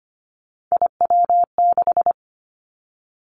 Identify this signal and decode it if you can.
Morse code, heard as a keyed tone, about 25 wpm, 700 Hz. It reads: IW6
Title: Three Challenging Letter-Number Combinations 25wpm